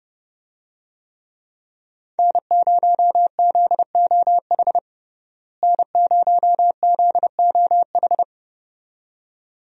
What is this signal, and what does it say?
Morse code, heard as a keyed tone, about 30 wpm, 700 Hz. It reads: N0ZOH N0ZOH
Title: USA Call Signs - 5 Characters - Repeated 2x - 30wpm